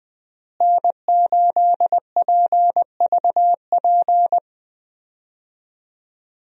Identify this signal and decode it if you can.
Morse code, heard as a keyed tone, about 20 wpm, 700 Hz. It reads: N8PVP
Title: USA Call Signs - 5 Characters - Repeated 3x - 20wpm